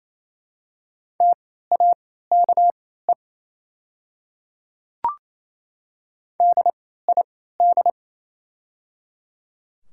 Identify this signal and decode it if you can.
Morse code, heard as a keyed tone, about 28 wpm, spaced as for 16 wpm, 700 Hz. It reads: TAKE DID
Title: 3x Character Spacing - Top 200 Words - 28wpm